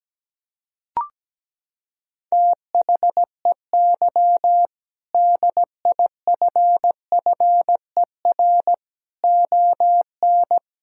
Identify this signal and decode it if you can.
Morse code, heard as a keyed tone, about 17 wpm, 700 Hz. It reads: THEY DIFFER ON
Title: Sentences from Top 1000 Words - Review Part 1 - 17wpm